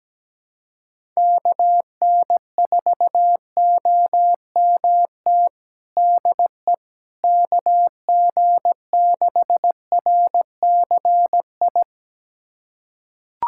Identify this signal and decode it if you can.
Morse code, heard as a keyed tone, about 17 wpm, 700 Hz. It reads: KN4OMT DE KG6RCI
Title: USA Callsign DE Callsign - 6 Characters - 17wpm